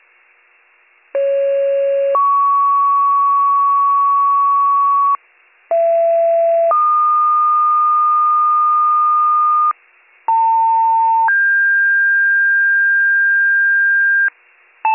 Two-tone page, heard as a one-second tone, then a three-second tone, about 550 then 1120 Hz, about 670 then 1220 Hz, about 900 then 1600 Hz. It is heard from 1.1 to 14.3 s.